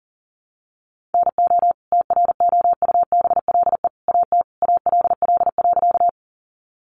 Morse code, heard as a keyed tone, about 40 wpm, 700 Hz. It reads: NO TROUBLE AT ALL.
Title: Sentences from Top 1200 Words 40wpm